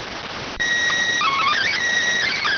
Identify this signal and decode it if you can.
SSTV (slow-scan television), PD 290.